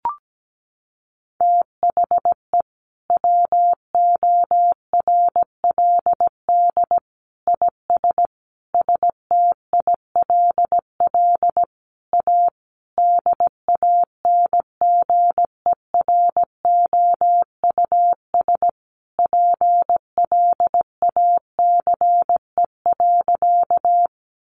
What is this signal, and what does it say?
Morse code, heard as a keyed tone, about 17 wpm, 700 Hz. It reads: THE WORLD IS STILL A DANGEROUS PLACE.
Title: Sentences from Top 1200 Words 17wpm